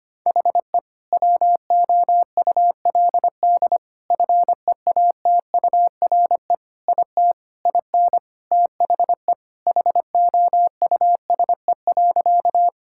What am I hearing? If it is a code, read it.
Morse code, heard as a keyed tone, about 25 wpm, 700 Hz. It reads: HE WOULD FEATURE IT IN THE HOUSE.
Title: Sentences from Top 1000 Words - Review Part 5 - 25wpm